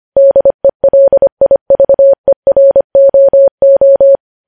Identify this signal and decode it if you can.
Morse code, heard as a keyed tone, about 25 wpm, 550 Hz. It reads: DELIVEROO
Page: news headlines in morse code at 25 wpm